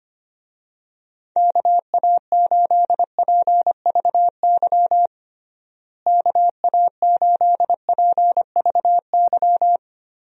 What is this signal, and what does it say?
Morse code, heard as a keyed tone, about 25 wpm, 700 Hz. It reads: KA8PVY KA8PVY
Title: USA Call Signs - 6 Characters - Repeated 3x - 25wpm